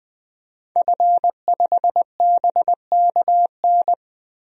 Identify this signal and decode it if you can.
Morse code, heard as a keyed tone, about 21 wpm, 700 Hz. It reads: F5BKN
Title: DXCC-100 Call Signs - 5 Characters - 20wpm